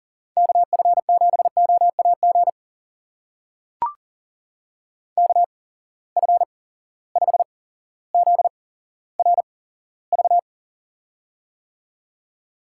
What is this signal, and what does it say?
Morse code, heard as a keyed tone, about 40 wpm, spaced as for 17 wpm, 700 Hz. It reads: KF7OAG KF5ZRV
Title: USA Call Signs - 6 Characters - 8x Character Spacing - 40wpm